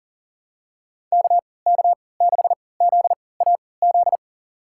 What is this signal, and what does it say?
Morse code, heard as a keyed tone, about 40 wpm, spaced as for 23 wpm, 700 Hz. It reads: KK6ZAZ
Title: USA Call Signs - 3x Character Spacing - 40wpm